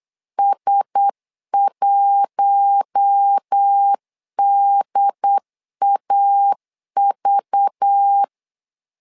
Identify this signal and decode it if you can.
Morse code, heard as a keyed tone, about 8 wpm, 800 Hz. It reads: S1DAV
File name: LEZIONI / 5 LBVC / 5 LBV 8wpm 2837